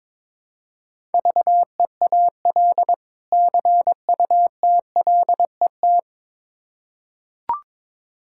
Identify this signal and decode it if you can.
Morse code, heard as a keyed tone, about 22 wpm, 700 Hz. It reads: VEAL CUTLET
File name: Difficult Words with Hints - 22wpm